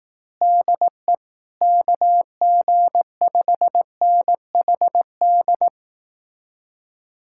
Morse code, heard as a keyed tone, about 18 wpm, 700 Hz. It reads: DE KG5NHD
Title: Speed-Racing Common QSO Phrases Encore - 15wpm